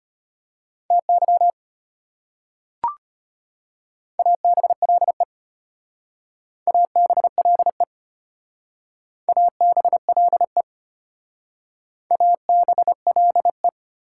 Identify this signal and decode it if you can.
Morse code, heard as a keyed tone, about 32 wpm, 700 Hz. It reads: TY ABLE ABLE ABLE ABLE